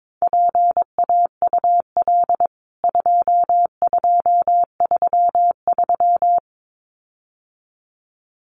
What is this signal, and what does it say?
Morse code, heard as a keyed tone, about 22 wpm, 700 Hz. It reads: PAUL 2233
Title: CWT Exchange 22wpm